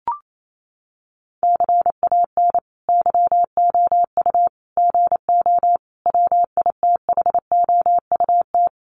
Morse code, heard as a keyed tone, about 28 wpm, 700 Hz. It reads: CAN YOU GO WITHOUT
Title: Sentences from Top 1500 Words 28wpm